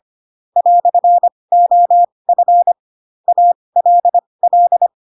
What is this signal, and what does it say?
Morse code, heard as a keyed tone, about 25 wpm, spaced as for 20 wpm, 700 Hz. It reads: "OF ALL
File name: Pinocchio / Pinocchio ch4 15 25 ik1hge